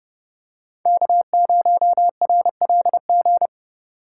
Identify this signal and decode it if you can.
Morse code, heard as a keyed tone, about 29 wpm, 700 Hz. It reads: K0RLG